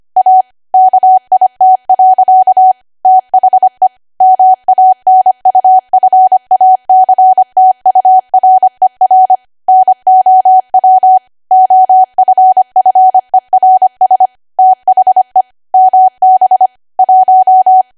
Morse code, heard as a keyed tone, about 25 wpm, 750 Hz. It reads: A KIT. THE MANUFACTURER NOW OFFERS THE MB 1
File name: cwa / W1AW-Level3 / 25WPM